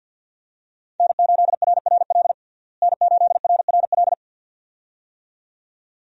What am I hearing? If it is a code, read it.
Morse code, heard as a keyed tone, about 47 wpm, 700 Hz. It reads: N8RRL N8RRL
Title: USA Call Signs - 5 Characters - Repeated 2x - 50wpm